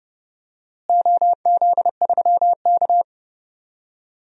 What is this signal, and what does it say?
Morse code, heard as a keyed tone, about 30 wpm, 700 Hz. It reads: OZ3K
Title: DXCC-100 Call Signs - 4 Characters - Repeated 2x - 30wpm